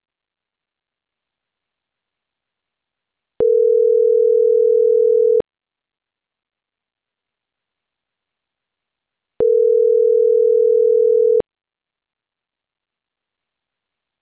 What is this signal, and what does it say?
Ringback tone (North America), 440 + 480 Hz.